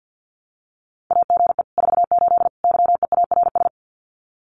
Morse code, heard as a keyed tone, about 48 wpm, 700 Hz. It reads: AGE 48 YEARS
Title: Common QSO Phrases Encore 50wpm